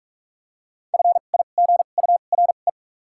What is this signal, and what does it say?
Morse code, heard as a keyed tone, about 46 wpm, spaced as for 33 wpm, 700 Hz. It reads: FIGURE
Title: 2x Character Spacing - Top 500 Words - 45wpm